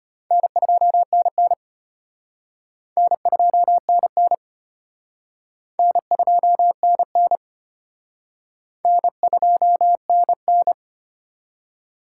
Morse code, heard as a keyed tone, about 33 wpm, 700 Hz. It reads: N2NN N2NN N2NN N2NN